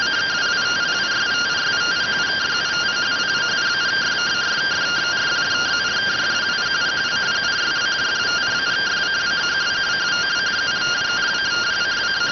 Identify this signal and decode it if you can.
Two-tone teleprinter signal, about 170 Hz shift.